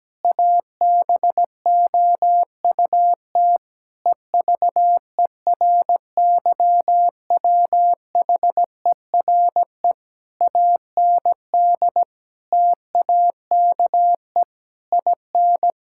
Morse code, heard as a keyed tone, about 17 wpm, 700 Hz. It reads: ABOUT EVERYWHERE AND TAKE IN
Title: Sentences from Top 1600 Words 17wpm